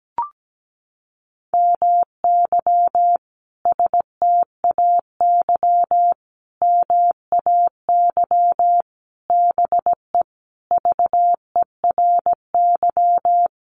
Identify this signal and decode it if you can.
Morse code, heard as a keyed tone, about 17 wpm, 700 Hz. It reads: MY STAY MAY BE VERY
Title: Sentences from Top 1000 Words 17wpm